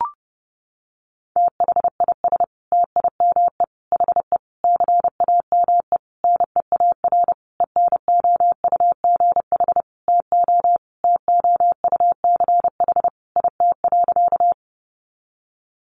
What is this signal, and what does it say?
Morse code, heard as a keyed tone, about 30 wpm, 700 Hz. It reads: THIS TIME HE CAME NEAR ENOUGH TO TOUCH IT.
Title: Sentences from Top 1200 Words 30wpm